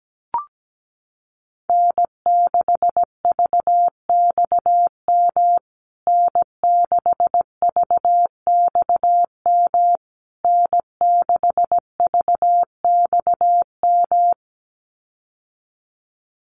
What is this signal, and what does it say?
Morse code, heard as a keyed tone, about 17 wpm, 700 Hz. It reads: N6VXM N6VXM N6VXM
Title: USA Call Signs Repeated 3x 17wpm